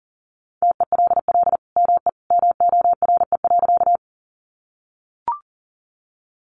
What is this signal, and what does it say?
Morse code, heard as a keyed tone, about 40 wpm, 700 Hz. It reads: TELL ME MORE.
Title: Sentences from Top 300 Words 40wpm